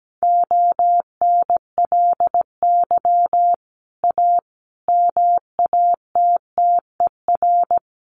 Morse code, heard as a keyed tone, about 17 wpm, 700 Hz. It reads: ONLY A MATTER
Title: Sets of 5 Words Encore 17wpm